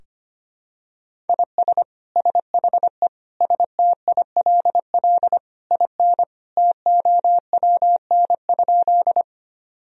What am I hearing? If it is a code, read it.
Morse code, heard as a keyed tone, about 25 wpm, 700 Hz. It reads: IS SHE STILL IN TOWN?
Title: Sentences from Top 1000 Words - Review Part 2 - 25wpm